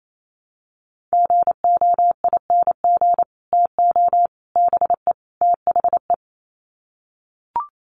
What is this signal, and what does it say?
Morse code, heard as a keyed tone, about 28 wpm, 700 Hz. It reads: GOING TO BE THE
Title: Sets of 4 Words 28wpm